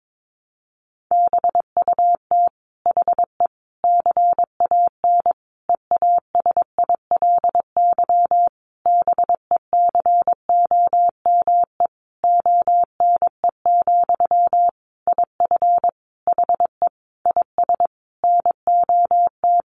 Morse code, heard as a keyed tone, about 22 wpm, 700 Hz. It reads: BUT HE CAN EASILY BECOME ONE, IF HE IS NOT
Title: Sentences from Top 900 Words 22wpm